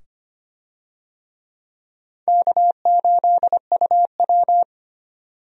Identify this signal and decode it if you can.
Morse code, heard as a keyed tone, about 25 wpm, 700 Hz. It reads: K8UW
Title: USA Call Signs - 4 Characters - 25wpm